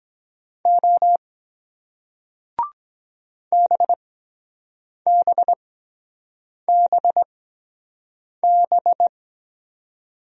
Morse code, heard as a keyed tone, about 23 wpm, spaced as for 7 wpm, 700 Hz. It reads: O BBBB